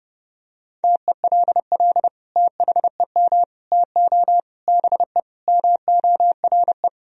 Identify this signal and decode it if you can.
Morse code, heard as a keyed tone, about 32 wpm, 700 Hz. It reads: TELL THEM TO BE MORE
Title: Sentences from Top 1000 Words - Review Part 3 - 30wpm